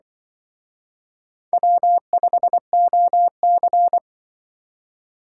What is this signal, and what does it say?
Morse code, heard as a keyed tone, about 24 wpm, 700 Hz. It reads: W5OC
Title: Speed Racing - USA 1x2 Call Signs 20wpm